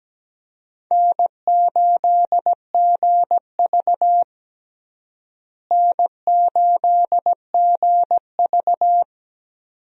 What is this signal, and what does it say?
Morse code, heard as a keyed tone, about 17 wpm, 700 Hz. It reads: N8GV N8GV
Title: USA Call Signs - 4 Characters - Repeated 2x - 17wpm